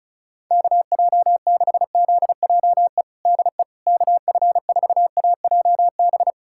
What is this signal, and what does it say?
Morse code, heard as a keyed tone, about 36 wpm, 700 Hz. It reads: KJ6ZJE DE KF4AJB
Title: USA Callsign DE Callsign 35wpm